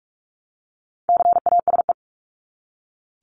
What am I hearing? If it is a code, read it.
Morse code, heard as a keyed tone, about 44 wpm, 700 Hz. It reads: CASE
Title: Top 501-750 Words 45wpm